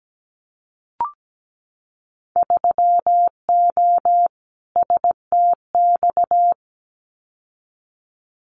Morse code, heard as a keyed tone, about 17 wpm, 700 Hz. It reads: 3O STX
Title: Winter Field Day Exchanges 17wpm